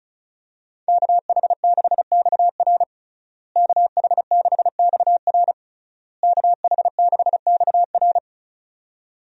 Morse code, heard as a keyed tone, about 34 wpm, 700 Hz. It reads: KH6XR KH6XR KH6XR
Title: DXCC-100 Call Signs - 5 Characters - Repeated 3x - 35wpm